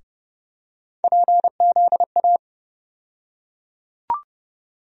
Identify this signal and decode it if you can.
Morse code, heard as a keyed tone, about 31 wpm, 700 Hz. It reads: PZA
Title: USA Call Sign Postfixes - 3 Letters 30wpm